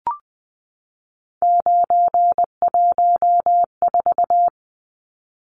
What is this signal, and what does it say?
Morse code, heard as a keyed tone, about 20 wpm, 700 Hz. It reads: 914
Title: Numbers - 3-Digit 20wpm